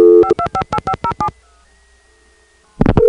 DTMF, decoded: 5258507